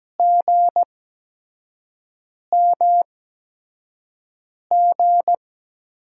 Morse code, heard as a keyed tone, about 17 wpm, spaced as for 5 wpm, 700 Hz. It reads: GMG
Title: USA Call Signs - 8x Character Spacing - 17wpm